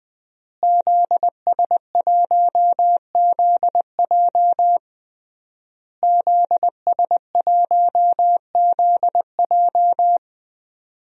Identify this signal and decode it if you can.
Morse code, heard as a keyed tone, about 20 wpm, 700 Hz. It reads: ZS1ZJ ZS1ZJ
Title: DXCC-100 Call Signs - 5 Characters - Repeated 3x - 20wpm